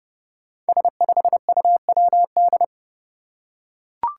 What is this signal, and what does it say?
Morse code, heard as a keyed tone, about 28 wpm, 700 Hz. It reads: S5UWD
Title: Five Challenging Letter-Number Combinations 30wpm